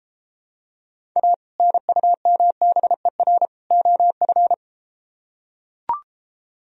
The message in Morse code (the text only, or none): A NUMBER OF